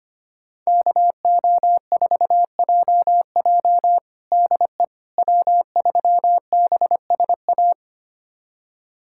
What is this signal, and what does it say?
Morse code, heard as a keyed tone, about 25 wpm, 700 Hz. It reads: KO4JJ DE W3BSA
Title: USA Callsign DE Callsign - 5 Characters - 25wpm